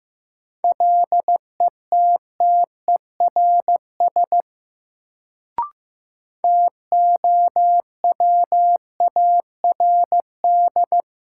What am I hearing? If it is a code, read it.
Morse code, heard as a keyed tone, about 15 wpm, 700 Hz. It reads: LETTERS TOWARD
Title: Sentences from Top 1000 Words - Review Part 6 - New Word Drill - 15wpm